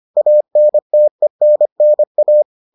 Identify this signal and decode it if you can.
Morse code, heard as a keyed tone, about 25 wpm, 600 Hz. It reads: ANTENNA